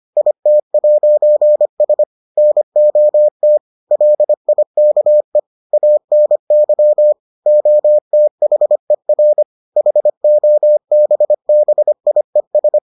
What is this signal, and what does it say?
Morse code, heard as a keyed tone, about 25 wpm, 600 Hz. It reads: IT'S NOT LIKE ANY OTHER HOBBIES